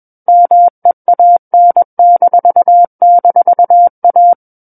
Morse code, heard as a keyed tone, about 21 wpm, 700 Hz. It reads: MEAN--A
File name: Book One / 12-Chapter 12 - 21 WPM